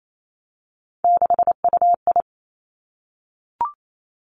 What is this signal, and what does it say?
Morse code, heard as a keyed tone, about 28 wpm, 700 Hz. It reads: 6UI
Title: Three Challenging Letter-Number Combinations 28wpm